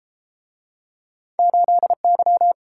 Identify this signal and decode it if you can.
Morse code, heard as a keyed tone, about 30 wpm, 700 Hz. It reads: 8Y